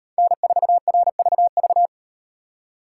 Morse code, heard as a keyed tone, about 38 wpm, 700 Hz. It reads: N4RVV